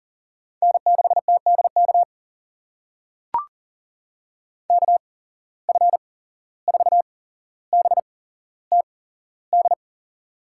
Morse code, heard as a keyed tone, about 42 wpm, spaced as for 16 wpm, 690 Hz. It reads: N6TDK KF4BTD
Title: USA Call Signs - 8x Character Spacing - 40wpm